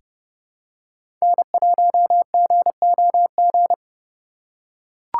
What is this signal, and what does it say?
Morse code, heard as a keyed tone, about 30 wpm, 700 Hz. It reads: N1GOG
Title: USA Call Signs - 3x Character Spacing - 30wpm